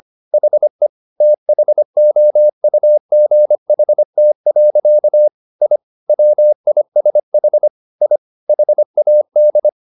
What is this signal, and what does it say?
Morse code, heard as a keyed tone, about 25 wpm, 600 Hz. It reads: HE THOUGHT. I WISH I HAD
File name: the old man and the sea part 1 25wpm